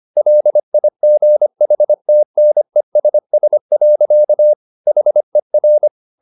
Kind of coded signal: Morse code